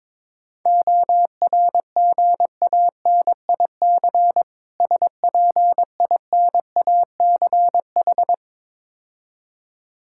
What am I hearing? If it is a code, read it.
Morse code, heard as a keyed tone, about 22 wpm, 690 Hz. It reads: ORGANIC SPINACH
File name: Difficult Words with Hints - 22wpm